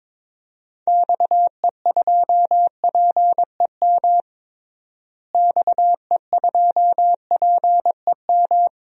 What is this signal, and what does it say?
Morse code, heard as a keyed tone, about 22 wpm, 700 Hz. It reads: XE2PEM XE2PEM
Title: DXCC-100 Call Signs - 6 Characters - Repeated 3x - 22wpm